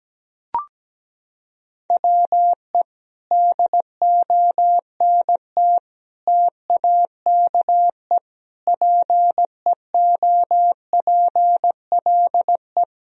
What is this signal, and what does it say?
Morse code, heard as a keyed tone, about 17 wpm, 700 Hz. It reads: WE DONT TAKE PEOPLE
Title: Sentences from Top 1000 Words - Review Part 4 - 17wpm